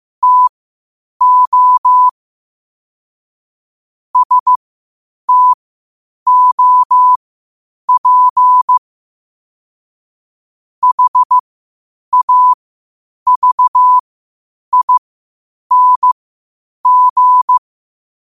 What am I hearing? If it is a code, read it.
Morse code, heard as a keyed tone, about 15 wpm, spaced as for 8 wpm, 1000 Hz. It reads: TO STOP HAVING
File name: qotdfiles / 8wpm